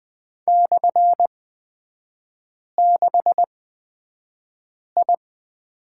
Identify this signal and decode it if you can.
Morse code, heard as a keyed tone, about 20 wpm, spaced as for 5 wpm, 700 Hz. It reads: /6I
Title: 243 Lesson 39e - Letter z - Three Characters 20wpm